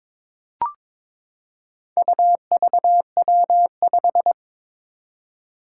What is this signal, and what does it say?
Morse code, heard as a keyed tone, about 22 wpm, 700 Hz. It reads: UVW5
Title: Four Challenging Letter-Number Combinations 22wpm